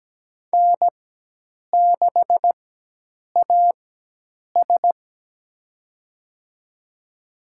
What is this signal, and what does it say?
Morse code, heard as a keyed tone, about 17 wpm, spaced as for 8 wpm, 700 Hz. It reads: N6AS
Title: USA Call Signs - 4 Characters - 4x Character Spacing - 17wpm